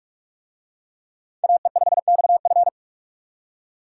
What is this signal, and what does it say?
Morse code, heard as a keyed tone, about 44 wpm, 700 Hz. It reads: AE5XF